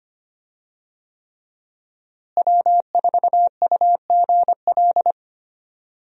Morse code, heard as a keyed tone, about 25 wpm, 700 Hz. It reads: W4UGL
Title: USA Call Signs - 5 Characters - Repeated 3x - 25wpm